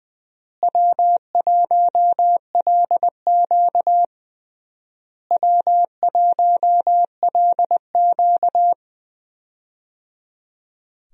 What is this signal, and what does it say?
Morse code, heard as a keyed tone, about 20 wpm, 700 Hz. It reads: W1LQ W1LQ
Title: USA Call Signs - 4 Characters - Repeated 3x - 20wpm